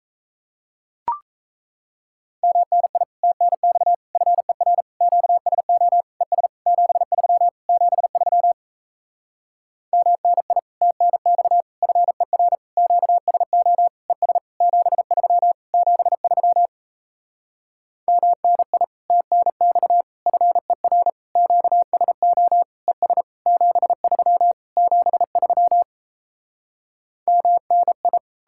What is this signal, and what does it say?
Morse code, heard as a keyed tone, about 37 wpm, 700 Hz. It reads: MNI TNX FER QSO ES 73 73 MNI TNX FER QSO ES 73 73 MNI TNX FER QSO ES 73 73 MNI